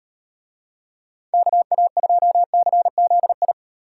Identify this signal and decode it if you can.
Morse code, heard as a keyed tone, about 38 wpm, 700 Hz. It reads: KA2CZI